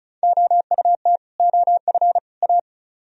Morse code, heard as a keyed tone, about 44 wpm, 700 Hz. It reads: OUT OF A